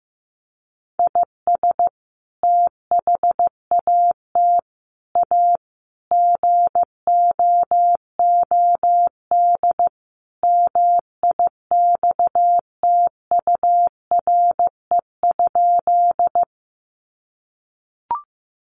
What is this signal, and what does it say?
Morse code, heard as a keyed tone, about 14 wpm, 700 Hz. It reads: IS THAT A GOOD MIXTURE?
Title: Sentences from Top 1500 Words 15wpm